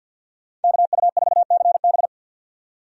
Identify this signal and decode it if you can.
Morse code, heard as a keyed tone, about 48 wpm, 700 Hz. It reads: KU4CB
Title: USA Call Signs - 5 Characters - 50wpm